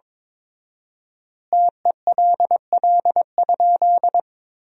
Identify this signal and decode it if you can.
Morse code, heard as a keyed tone, about 22 wpm, 700 Hz. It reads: TELL?
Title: 4x Word-Spacing - Sentences from Top 300 Words - 22wpm